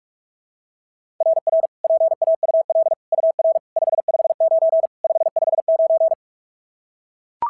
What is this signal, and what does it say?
Morse code, heard as a keyed tone, about 44 wpm, 690 Hz. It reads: RR PAUL UR 559 559
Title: SOTA QSO Practice 45wpm